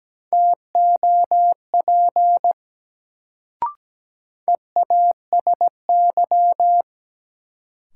Morse code, heard as a keyed tone, about 17 wpm, 700 Hz. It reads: TOP EASY
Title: Sentences from Top 700 Words - New Word Drill - 17wpm